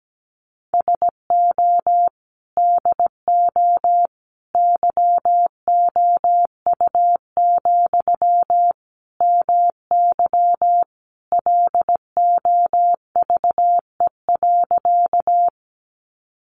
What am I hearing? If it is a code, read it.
Morse code, heard as a keyed tone, about 17 wpm, 700 Hz. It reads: SO DO YOU, MY LOVE.
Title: Sentences from Top 400 Words 17wpm